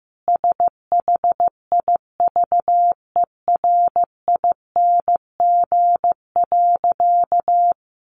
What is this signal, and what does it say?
Morse code, heard as a keyed tone, about 14 wpm, 700 Hz. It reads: SHIVERING.
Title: A Sound Of Thunder 15wpm